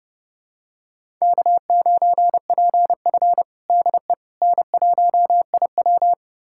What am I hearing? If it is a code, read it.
Morse code, heard as a keyed tone, about 32 wpm, 700 Hz. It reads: K9PF DE N1IW
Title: USA Callsign DE Callsign - 4 Characters - 30wpm